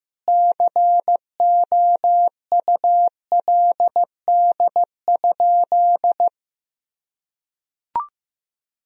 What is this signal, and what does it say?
Morse code, heard as a keyed tone, about 16 wpm, 700 Hz. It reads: COULD?
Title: Sentences from Top 400 Words and Abbreviations 15wpm